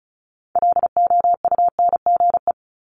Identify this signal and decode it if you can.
Morse code, heard as a keyed tone, about 36 wpm, 700 Hz. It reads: LOUNGE
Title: Multi-letter Phonemes 35wpm